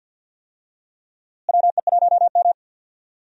Morse code, heard as a keyed tone, about 46 wpm, 700 Hz. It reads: WE1M
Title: USA Call Signs - 4 Characters - Repeated 3x - 50wpm